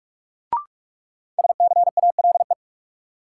Morse code, heard as a keyed tone, about 41 wpm, 700 Hz. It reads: SCALE
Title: Sentences from Top 1000 Words - Review Part 4 - New Word Drill - 45wpm